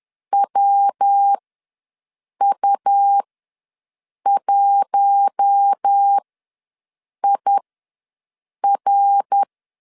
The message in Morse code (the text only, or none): WU1IR